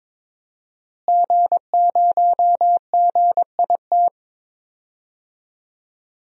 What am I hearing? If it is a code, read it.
Morse code, heard as a keyed tone, about 22 wpm, 700 Hz. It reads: G0GIT
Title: DXCC-100 Call Signs - 5 Characters - Repeated 2x - 22wpm